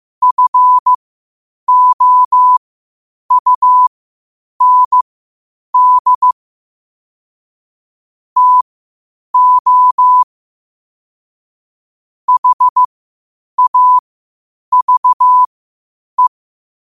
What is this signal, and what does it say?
Morse code, heard as a keyed tone, about 15 wpm, spaced as for 8 wpm, 1000 Hz. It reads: FOUND TO HAVE
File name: qotdfiles / 8wpm